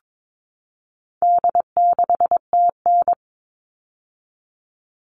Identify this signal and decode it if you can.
Morse code, heard as a keyed tone, about 22 wpm, 700 Hz. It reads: D6TN
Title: All Cut Numbers - 4-Digit 22wpm